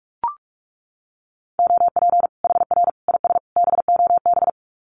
Morse code, heard as a keyed tone, about 44 wpm, 700 Hz. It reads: OP HR IS BOB
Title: Common QSO Phrases 45wpm